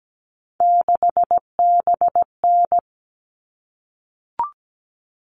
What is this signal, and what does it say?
Morse code, heard as a keyed tone, about 17 wpm, 700 Hz. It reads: 6BN
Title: All Cut Numbers - 3-Digit 17wpm